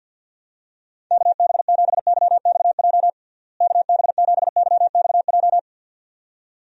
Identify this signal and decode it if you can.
Morse code, heard as a keyed tone, about 50 wpm, 700 Hz. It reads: KB7YXJ KB7YXJ
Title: USA Call Signs - 6 Characters - Repeated 2x - 50wpm